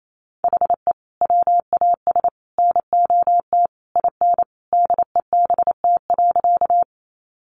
Morse code, heard as a keyed tone, about 28 wpm, 700 Hz. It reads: HE WAS NOT IN DEBT.
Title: Sentences from Top 1300 Words 28wpm